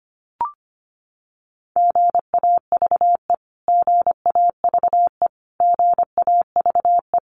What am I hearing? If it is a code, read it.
Morse code, heard as a keyed tone, about 25 wpm, 700 Hz. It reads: GAVE GAVE GAVE
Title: Top 500 Words Repeated 25wpm